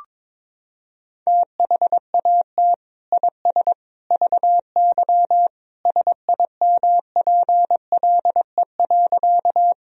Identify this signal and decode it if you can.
Morse code, heard as a keyed tone, about 22 wpm, 700 Hz. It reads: THAT IS VY SIMPLE.
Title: Sentences from Top 400 Words and Abbreviations 22wpm